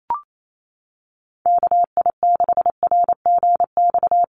Morse code, heard as a keyed tone, about 28 wpm, 700 Hz. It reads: KI6RGX
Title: USA Call Signs Repeated 2x 28wpm